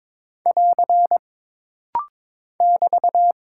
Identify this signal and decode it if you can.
Morse code, heard as a keyed tone, about 23 wpm, 700 Hz. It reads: <AR> <BT>
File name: practice / 22wpm / Single Punctuation-Prosign - Rapid-Fire - 22wpm